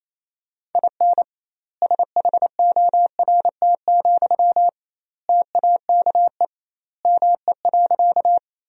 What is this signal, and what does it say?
Morse code, heard as a keyed tone, about 28 wpm, 700 Hz. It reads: IN SHORT, TAKE ME.
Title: 2x Word-Spacing - Sentences from Top 400 Words - 28wpm